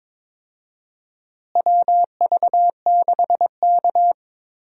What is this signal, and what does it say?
Morse code, heard as a keyed tone, about 22 wpm, 700 Hz. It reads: WV6K
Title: USA Call Signs - 4 Characters - Repeated 2x - 22wpm